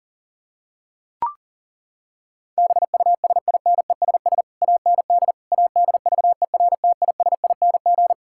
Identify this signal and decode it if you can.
Morse code, heard as a keyed tone, about 40 wpm, 700 Hz. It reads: BUSINESS AND ADVERTISING
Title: Sentences from Top 1000 Words - Review Part 6 - 40wpm